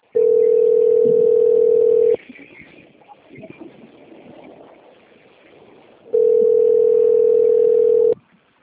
Ringback tone (North America), 440 + 480 Hz.
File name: out- - - - - .26666